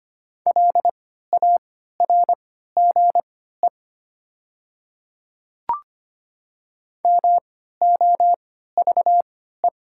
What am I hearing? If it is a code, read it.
Morse code, heard as a keyed tone, about 25 wpm, spaced as for 14 wpm, 700 Hz. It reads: LARGE MOVE
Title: 3x Character Spacing - Top 300 Words - 25wpm